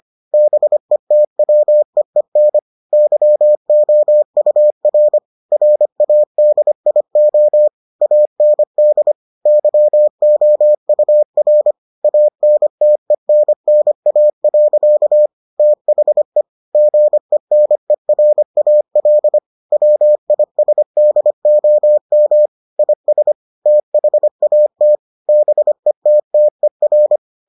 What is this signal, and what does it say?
Morse code, heard as a keyed tone, about 25 wpm, 600 Hz. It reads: BETWEEN YOUR RADIO AND YOUR ANTENNA. THE GENERAL WISDOM IS THAT BETTER